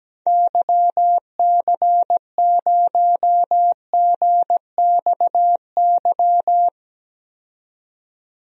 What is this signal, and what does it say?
Morse code, heard as a keyed tone, about 17 wpm, 700 Hz. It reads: YC0GXY
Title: DXCC-100 Call Signs - 6 Characters - 17wpm